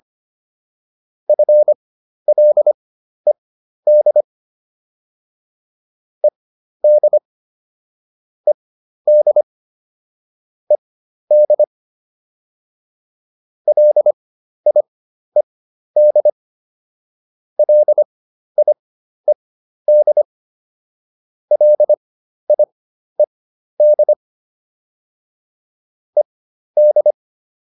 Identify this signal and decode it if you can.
Morse code, heard as a keyed tone, about 25 wpm, spaced as for 12 wpm, 600 Hz. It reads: FLED ED ED ED LIED LIED LIED ED